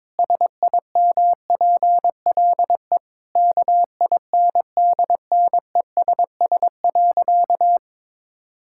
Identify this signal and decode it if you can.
Morse code, heard as a keyed tone, about 22 wpm, 700 Hz. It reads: SIMPLE KINDNESS.
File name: Difficult Sentences with Hints - 22wpm